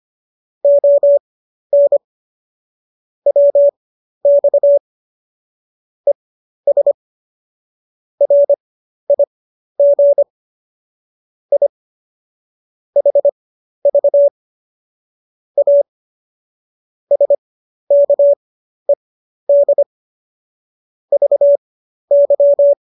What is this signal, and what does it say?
Morse code, heard as a keyed tone, about 25 wpm, spaced as for 12 wpm, 570 Hz. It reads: ON WX ES RIG I HV A SKED VY